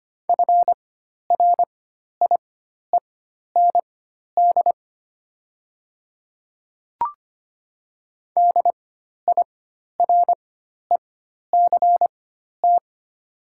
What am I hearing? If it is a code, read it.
Morse code, heard as a keyed tone, about 25 wpm, spaced as for 12 wpm, 690 Hz. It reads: FRIEND DIRECT
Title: 4x Character Spacing - Top 500 Words - 25wpm